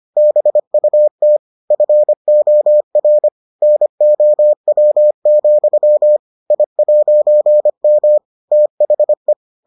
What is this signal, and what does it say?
Morse code, heard as a keyed tone, about 25 wpm, 600 Hz. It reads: BUT FOR NOW, I'M THE